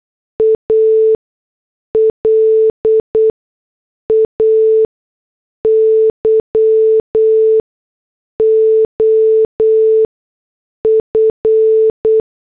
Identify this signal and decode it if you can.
Morse code, heard as a keyed tone, about 8 wpm, spaced as for 6 wpm, 440 Hz. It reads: A LAY OF